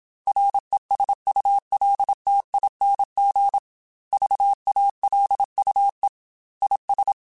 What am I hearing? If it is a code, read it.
Morse code, heard as a keyed tone, about 27 wpm, 800 Hz. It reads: RESULTING VALUE IS